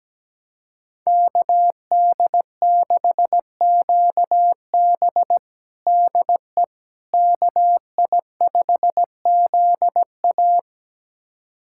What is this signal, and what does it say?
Morse code, heard as a keyed tone, about 17 wpm, 700 Hz. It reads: KD6QB DE KI5ZA
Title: USA Callsign DE Callsign - 5 Characters - 17wpm